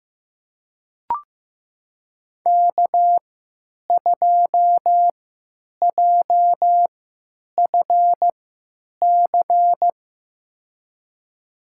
Morse code, heard as a keyed tone, about 15 wpm, spaced as for 9 wpm, 700 Hz. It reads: K2JFC
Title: USA Call Signs - 5 Characters - 3x Character Spacing - 15wpm